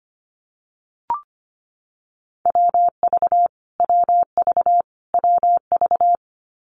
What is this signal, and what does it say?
Morse code, heard as a keyed tone, about 25 wpm, 700 Hz. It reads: WV WV WV
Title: US States Repeated 3x 25wpm